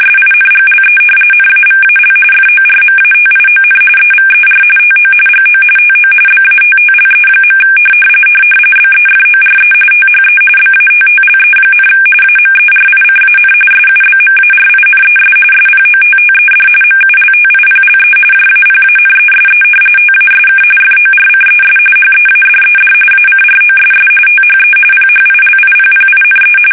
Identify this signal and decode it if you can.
RTTY, about 45.45 baud. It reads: SINGLE NAVAL FORMATION EVER TO GO TO BATTLE. BEFORE THE BATTLE, AMERICAN SUBMARINES SPOTTED MULTIPLE JAPANESE TASK FORCES HEADED ACROSS THE PHILIPPINE
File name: fsk850 / battleshipnj